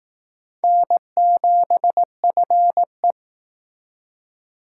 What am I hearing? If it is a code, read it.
Morse code, heard as a keyed tone, about 19 wpm, 700 Hz. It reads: N7FE